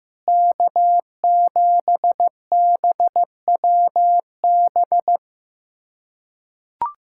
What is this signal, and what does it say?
Morse code, heard as a keyed tone, about 15 wpm, 700 Hz. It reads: K7BWB
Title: USA Callsign DE Callsign 15wpm